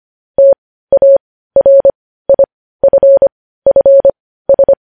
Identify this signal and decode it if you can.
Morse code, heard as a keyed tone, about 25 wpm, spaced as for 15 wpm, 550 Hz. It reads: TARIFFS
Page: news headlines in morse code at 15 wpm